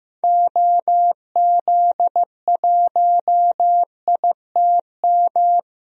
Morse code, heard as a keyed tone, about 15 wpm, 690 Hz. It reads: OZ1ITM
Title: DXCC-100 Call Signs - 6 Characters - 15wpm